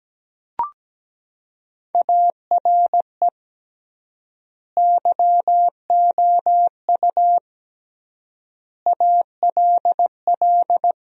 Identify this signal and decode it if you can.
Morse code, heard as a keyed tone, about 17 wpm, 700 Hz. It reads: ARE YOU ALL
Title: 3x Word-Spacing - Sentences from Top 400 Words - 17wpm